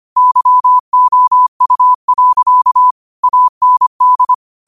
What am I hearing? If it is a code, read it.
Morse code, heard as a keyed tone, about 25 wpm, 1000 Hz. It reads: YOU. AND
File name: qotdfiles / 25wpm